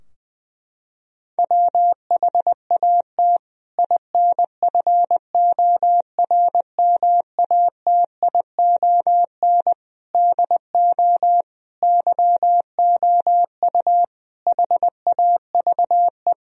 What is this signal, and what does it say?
Morse code, heard as a keyed tone, about 20 wpm, 700 Hz. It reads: WHAT INFORMATION DO YOU HAVE
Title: Sentences from Top 1000 Words - Review Part 3 - 20wpm